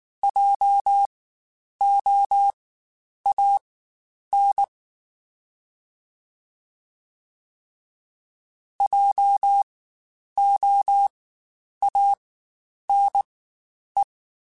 Morse code, heard as a keyed tone, about 19 wpm, spaced as for 9 wpm, 800 Hz. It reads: JOAN JOANE
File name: Spaced Alphas / Spaced Alphas mp3s / Js sp @18